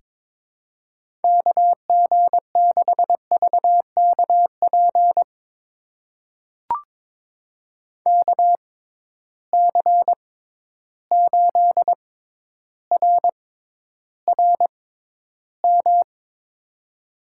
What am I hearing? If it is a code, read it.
Morse code, heard as a keyed tone, about 23 wpm, spaced as for 11 wpm, 700 Hz. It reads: KG6VKP KC8RRM
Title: USA Call Signs - 6 Characters - 6x Character Spacing - 22wpm